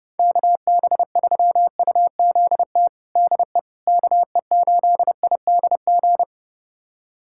Morse code, heard as a keyed tone, about 30 wpm, 700 Hz. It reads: KB3UZT DE KE8IDG
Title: USA Callsign DE Callsign 30wpm